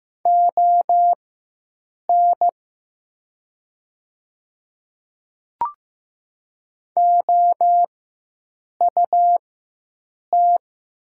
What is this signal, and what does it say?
Morse code, heard as a keyed tone, about 15 wpm, spaced as for 7 wpm, 700 Hz. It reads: ON OUT
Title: 4x Character Spacing - Top 100 Words - 15wpm